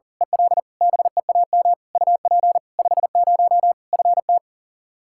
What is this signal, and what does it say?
Morse code, heard as a keyed tone, about 37 wpm, 700 Hz. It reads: EL BEAM UP 50 FT
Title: Practice QSO - Exchange 2 - Info - 40wpm